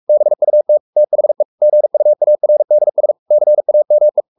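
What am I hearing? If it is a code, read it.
Morse code, heard as a keyed tone, about 44 wpm, 600 Hz. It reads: BUT THE GUARDS CAME
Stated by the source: Morse code at 43 wpm sending Ken Follett's novel Never in English, from CW med Gnister.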